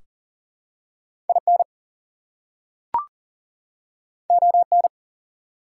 Morse code, heard as a keyed tone, about 40 wpm, 700 Hz. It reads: IN ON
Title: Two Letter Words - 40wpm